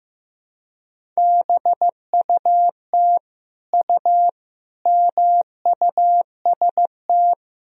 Morse code, heard as a keyed tone, about 15 wpm, 700 Hz. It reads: BUT U MUST
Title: Sentences from Top 300 Words and Abbreviations 15wpm